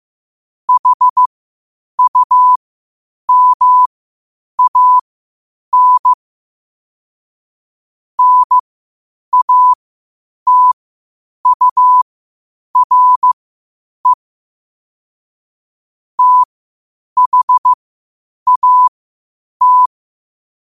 Morse code, heard as a keyed tone, about 15 wpm, spaced as for 8 wpm, 1000 Hz. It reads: HUMAN NATURE THAT
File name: qotdfiles / 8wpm